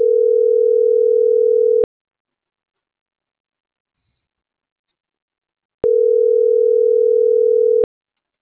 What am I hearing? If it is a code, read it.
Ringback tone (North America), 440 + 480 Hz.